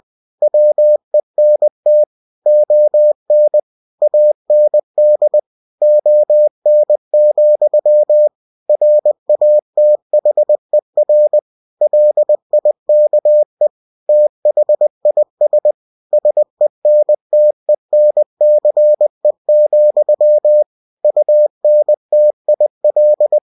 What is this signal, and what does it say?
Morse code, heard as a keyed tone, about 20 wpm, 600 Hz. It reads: WENT ON AND ON, RATHER LIKE THIS SENTENCE, UNTIL
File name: cw-books / winnie-the-pooh / winnie-20wpm-page-0101